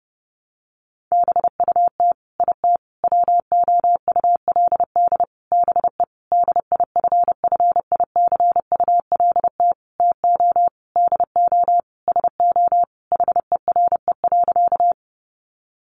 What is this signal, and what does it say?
Morse code, heard as a keyed tone, about 30 wpm, 700 Hz. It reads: BUT IT WOULD BE DIFFICULT TO DO SO HERE.
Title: Sentences from Top 600 Words 30wpm